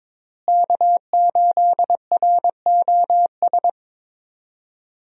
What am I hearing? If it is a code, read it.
Morse code, heard as a keyed tone, about 23 wpm, 700 Hz. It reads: K8ROS
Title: USA Call Signs - 5 Characters - Repeated 3x - 22wpm